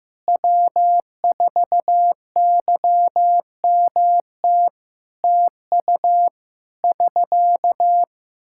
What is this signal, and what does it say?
Morse code, heard as a keyed tone, about 15 wpm, 700 Hz. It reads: W4YMT TU <SK>
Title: Practice QSO - Exchange 4 - Ending - 15wpm